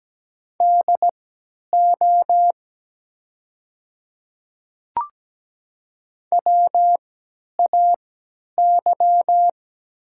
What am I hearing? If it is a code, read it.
Morse code, heard as a keyed tone, about 17 wpm, spaced as for 10 wpm, 700 Hz. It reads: DO WAY
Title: 3x Character Spacing - Top 200 Words - 17wpm